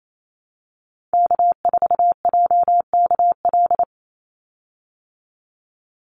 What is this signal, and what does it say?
Morse code, heard as a keyed tone, about 28 wpm, 700 Hz. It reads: K4JKL
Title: Call Signs Repeated 2x - 28wpm